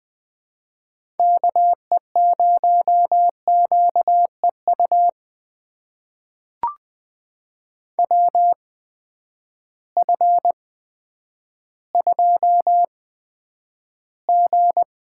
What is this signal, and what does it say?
Morse code, heard as a keyed tone, about 20 wpm, spaced as for 10 wpm, 700 Hz. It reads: KE0QEU WF2G
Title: USA Call Signs - 8x Character Spacing - 20wpm